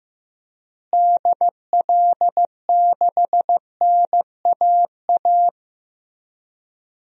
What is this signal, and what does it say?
Morse code, heard as a keyed tone, about 15 wpm, 700 Hz. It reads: DL6NAA
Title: DXCC-100 Call Signs - 6 Characters - Repeated 3x - 15wpm